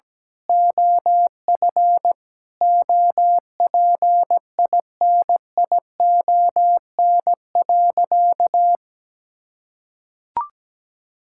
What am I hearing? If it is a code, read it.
Morse code, heard as a keyed tone, about 17 wpm, 700 Hz. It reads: OF OPINION.
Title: Sentences from Top 1600 Words 17wpm